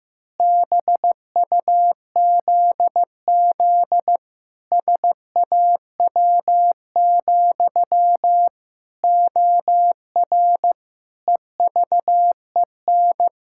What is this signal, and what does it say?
Morse code, heard as a keyed tone, about 15 wpm, 700 Hz. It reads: BUZZ SAW, OR EVEN
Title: Pangram Challenge v2 - 15wpm